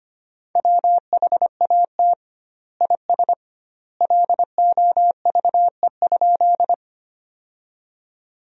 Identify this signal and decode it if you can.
Morse code, heard as a keyed tone, about 25 wpm, 700 Hz. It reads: WHAT IS LOVE?
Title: 2x Word-Spacing - Sentences from Top 400 Words - 25wpm